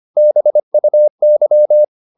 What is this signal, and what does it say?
Morse code, heard as a keyed tone, about 25 wpm, 600 Hz. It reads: BUY